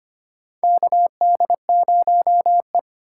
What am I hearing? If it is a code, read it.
Morse code, heard as a keyed tone, about 25 wpm, 700 Hz. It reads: KD0E
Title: USA Call Signs - 4 Characters - 2x Character Spacing - 25wpm